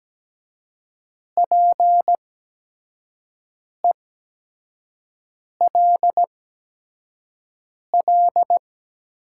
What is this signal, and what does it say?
Morse code, heard as a keyed tone, about 17 wpm, spaced as for 5 wpm, 700 Hz. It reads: PELL
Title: 8x Character Spacing - Top 200 Words - 17wpm